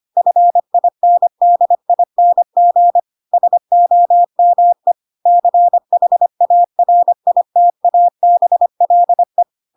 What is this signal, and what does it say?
Morse code, heard as a keyed tone, about 25 wpm, 700 Hz. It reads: FINDING SOME CHARITABLE
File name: Pinocchio / Pinocchio ch5 25 25 ik1hge